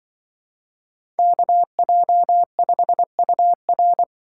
Morse code, heard as a keyed tone, about 25 wpm, 700 Hz. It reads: KJ5UR
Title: Speed Racing - USA 2x2 Call Signs 20wpm